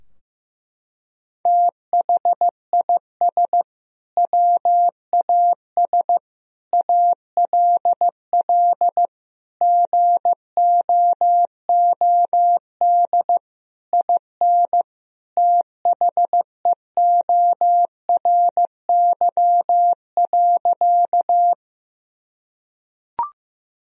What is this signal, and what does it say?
Morse code, heard as a keyed tone, about 15 wpm, 700 Hz. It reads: THIS WAS ALL GOOD IN THEORY.
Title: Sentences from Top 1000 Words - Review Part 3 - 15wpm